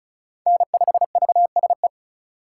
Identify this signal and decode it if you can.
Morse code, heard as a keyed tone, about 36 wpm, 700 Hz. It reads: N5VSE
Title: USA Call Signs - 5 Characters - 6x Character Spacing - 35wpm